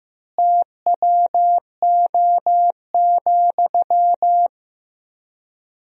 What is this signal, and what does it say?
Morse code, heard as a keyed tone, about 15 wpm, 700 Hz. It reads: TWO,
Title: 3x Word-Spacing - Sentences from Top 200 Words - 15wpm